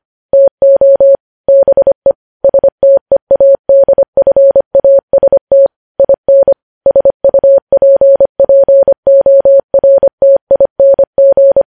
Morse code, heard as a keyed tone, about 25 wpm, 550 Hz. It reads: TO BE STEADFAST IN SUPPORTING
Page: news headlines in morse code at 25 wpm